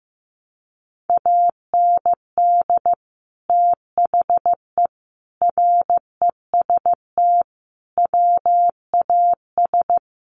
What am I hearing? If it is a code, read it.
Morse code, heard as a keyed tone, about 15 wpm, 700 Hz. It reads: AND THE REST WAS
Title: Sentences from Top 1000 Words 15wpm